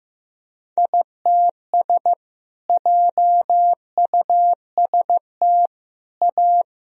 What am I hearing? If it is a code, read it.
Morse code, heard as a keyed tone, about 16 wpm, 700 Hz. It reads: ITS JUST A
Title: Sentences from Top 1000 Words - Review Part 1 - 15wpm